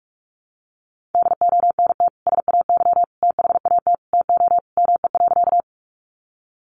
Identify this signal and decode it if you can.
Morse code, heard as a keyed tone, about 44 wpm, 700 Hz. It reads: DONT SAY THAT TO ME.
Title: Sentences from Top 300 Words 45wpm